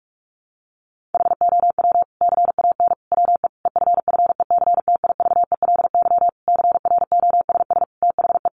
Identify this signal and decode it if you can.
Morse code, heard as a keyed tone, about 44 wpm, 700 Hz. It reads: HOW CAN WE EFFECTIVELY CROSS THE
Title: Sentences from Top 1400 Words 45wpm